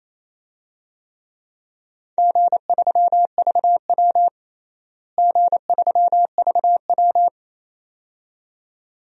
Morse code, heard as a keyed tone, about 28 wpm, 700 Hz. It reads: G3VW G3VW
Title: DXCC-100 Call Signs - 4 Characters - Repeated 2x - 28wpm